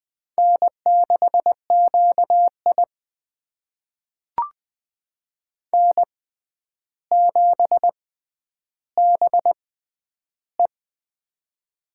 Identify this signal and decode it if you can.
Morse code, heard as a keyed tone, about 20 wpm, spaced as for 10 wpm, 700 Hz. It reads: N6QI N7BE
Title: USA Call Signs - 6x Character Spacing - 20wpm